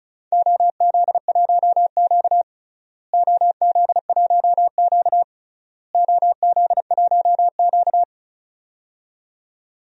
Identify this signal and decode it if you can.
Morse code, heard as a keyed tone, about 36 wpm, 700 Hz. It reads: OZ1Q OZ1Q OZ1Q
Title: DXCC-100 Call Signs - 4 Characters - Repeated 3x - 35wpm